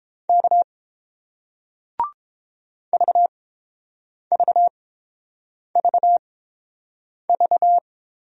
Morse code, heard as a keyed tone, about 30 wpm, spaced as for 7 wpm, 700 Hz. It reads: K VVVV